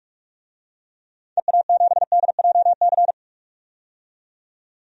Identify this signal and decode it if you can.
Morse code, heard as a keyed tone, about 46 wpm, 700 Hz. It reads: EA7DJC
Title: DXCC-100 Call Signs - 6 Characters - 45wpm